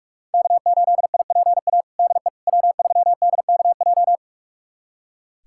Morse code, heard as a keyed tone, about 45 wpm, 700 Hz. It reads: K8IPA DE W3DKJ
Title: USA Callsign DE Callsign - 5 Characters - 45wpm